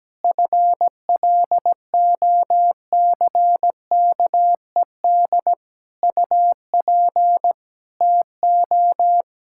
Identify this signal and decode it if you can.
Morse code, heard as a keyed tone, about 17 wpm, 700 Hz. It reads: FLOCKED UP TO
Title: Pangram Challenge v2 - 17wpm